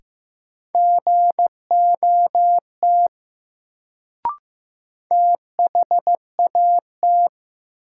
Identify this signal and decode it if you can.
Morse code, heard as a keyed tone, about 16 wpm, 700 Hz. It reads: GOT THAT
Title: Sets of 2 Words - New Word Drill - 15wpm